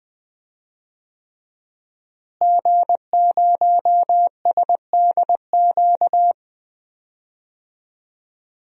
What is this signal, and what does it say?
Morse code, heard as a keyed tone, about 20 wpm, 700 Hz. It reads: G0SDQ
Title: DXCC-100 Call Signs - 5 Characters - Repeated 3x - 20wpm